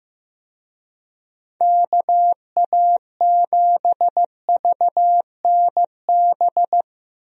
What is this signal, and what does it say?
Morse code, heard as a keyed tone, about 16 wpm, 700 Hz. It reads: KA7VNB
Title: USA Call Signs - 6 Characters - Repeated 3x - 15wpm